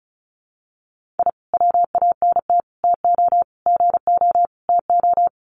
Morse code, heard as a keyed tone, about 34 wpm, 700 Hz. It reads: I WANT TO GO TO
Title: Sets of 5 Words Encore 35wpm